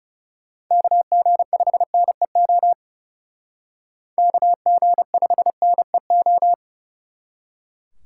Morse code, heard as a keyed tone, about 36 wpm, 700 Hz. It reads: KG5NEO KG5NEO
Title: Speed Racing - USA 2x3 Call Signs 30wpm